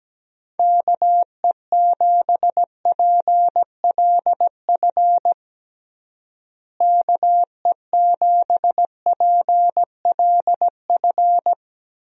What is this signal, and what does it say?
Morse code, heard as a keyed tone, about 17 wpm, 700 Hz. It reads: KE7PLF KE7PLF
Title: USA Call Signs - 6 Characters - Repeated 3x - 17wpm